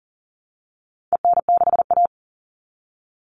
Morse code, heard as a keyed tone, about 40 wpm, 700 Hz. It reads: EN6A
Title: All Cut Numbers - 4-Digit 40wpm